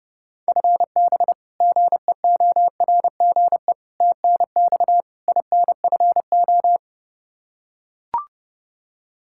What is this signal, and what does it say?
Morse code, heard as a keyed tone, about 30 wpm, 700 Hz. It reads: FB GEORGE TNX INFO
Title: K1USN SST QSO Practice 30wpm